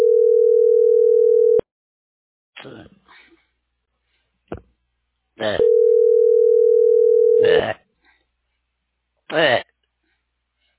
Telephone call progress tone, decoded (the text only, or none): ringback tone